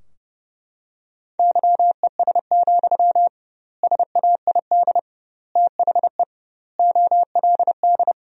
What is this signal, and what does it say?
Morse code, heard as a keyed tone, about 30 wpm, 700 Hz. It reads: YES, SAID THE OLD
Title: 2x Word-Spacing - Sentences from Top 500 Words - 30wpm